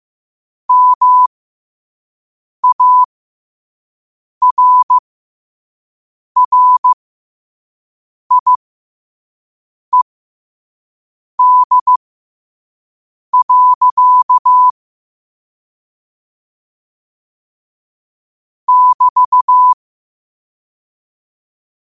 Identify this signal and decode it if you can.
Morse code, heard as a keyed tone, about 15 wpm, spaced as for 5 wpm, 1000 Hz. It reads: MARRIED. <BT>
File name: qotdfiles / 5wpm